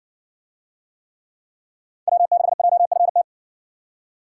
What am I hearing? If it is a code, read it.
Morse code, heard as a keyed tone, about 59 wpm, 700 Hz. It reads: W6JFT